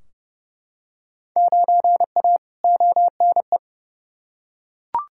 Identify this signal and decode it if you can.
Morse code, heard as a keyed tone, about 32 wpm, 700 Hz. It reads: 9A ONE
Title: Field Day Exchanges Encore - 30wpm